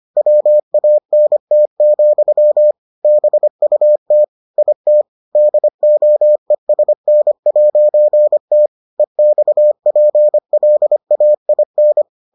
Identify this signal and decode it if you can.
Morse code, heard as a keyed tone, about 25 wpm, 600 Hz. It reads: WANT, BUT IT DOESN'T EXPLAIN